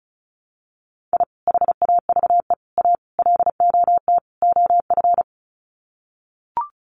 Morse code, heard as a keyed tone, about 34 wpm, 700 Hz. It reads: I HAVE A LOT OF
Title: Sets of 5 Words 35wpm